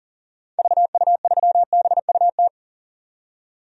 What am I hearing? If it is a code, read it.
Morse code, heard as a keyed tone, about 39 wpm, 700 Hz. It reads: VU3BUT
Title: DXCC-100 Call Signs - 6 Characters - 40wpm